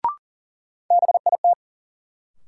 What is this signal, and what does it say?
Morse code, heard as a keyed tone, about 40 wpm, 700 Hz. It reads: BIT
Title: Sets of 3 Words - New Word Drill - 40wpm